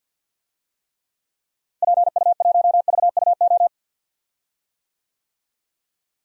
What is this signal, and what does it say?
Morse code, heard as a keyed tone, about 52 wpm, 700 Hz. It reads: PU1VUO
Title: DXCC-100 Call Signs - 6 Characters - Repeated 3x - 50wpm